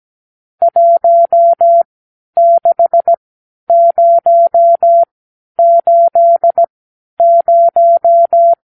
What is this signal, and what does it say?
Morse code, heard as a keyed tone, about 17 wpm, spaced as for 10 wpm, 680 Hz. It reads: 16080
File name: arrl / Review (1-0)